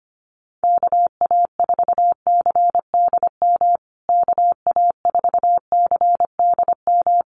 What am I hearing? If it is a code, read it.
Morse code, heard as a keyed tone, about 25 wpm, 700 Hz. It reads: KA4CDM KA4CDM
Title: USA Call Signs Repeated 2x 25wpm